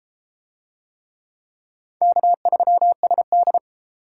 Morse code, heard as a keyed tone, about 32 wpm, 700 Hz. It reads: K3SD